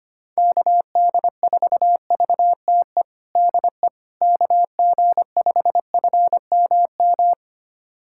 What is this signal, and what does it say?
Morse code, heard as a keyed tone, about 25 wpm, 700 Hz. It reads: KD4VTE DE KG5FMM
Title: USA Callsign DE Callsign - 6 Characters - 25wpm